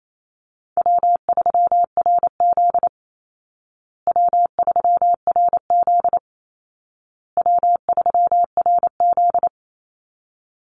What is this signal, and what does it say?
Morse code, heard as a keyed tone, about 28 wpm, 700 Hz. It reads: W3RZ W3RZ W3RZ
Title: Easy Call Signs Repeated 3x - 28wpm